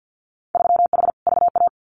Morse code, heard as a keyed tone, about 50 wpm, 700 Hz. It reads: 3H VA
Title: Winter Field Day Exchanges Encore 50wpm